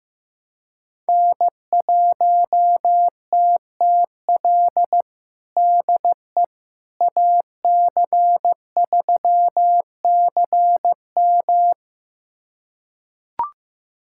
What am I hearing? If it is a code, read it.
Morse code, heard as a keyed tone, about 15 wpm, 700 Hz. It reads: N1TTL DE AC3CM
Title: USA Callsign DE Callsign - 5 Characters - 15wpm